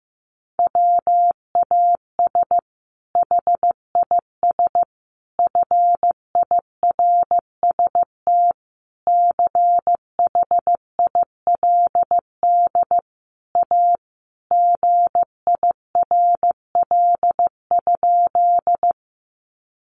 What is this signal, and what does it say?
Morse code, heard as a keyed tone, about 15 wpm, 700 Hz. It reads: WAS HIS FIRST CHILD A GIRL?
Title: Sentences from Top 800 Words 15wpm